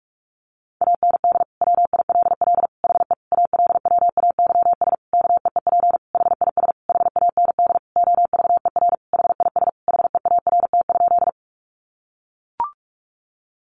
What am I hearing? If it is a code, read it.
Morse code, heard as a keyed tone, about 44 wpm, 700 Hz. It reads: AND WILL HE ALWAYS KEEP HIS HAND OVER HIS HEART?
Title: Sentences from Top 1200 Words 45wpm